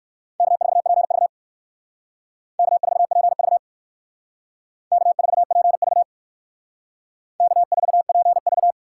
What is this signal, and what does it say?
Morse code, heard as a keyed tone, about 58 wpm, 700 Hz. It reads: K4PV K4PV K4PV K4PV